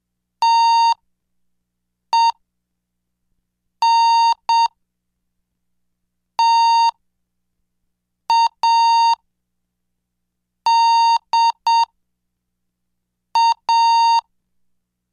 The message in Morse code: TENTADA